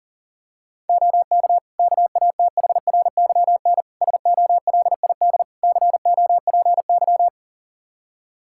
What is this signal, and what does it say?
Morse code, heard as a keyed tone, about 37 wpm, 700 Hz. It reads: OK KATHRYN SOLID COPY
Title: K1USN SST QSO Practice 40wpm